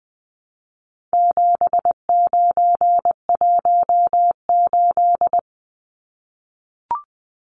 Morse code, heard as a keyed tone, about 20 wpm, 700 Hz. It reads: 7918
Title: Numbers - 4-Digit 20wpm